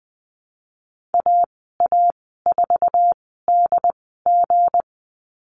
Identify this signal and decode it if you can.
Morse code, heard as a keyed tone, about 20 wpm, spaced as for 15 wpm, 700 Hz. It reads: AA4DG
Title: 186 Lesson 30g - Letter b - Call Signs 20wpm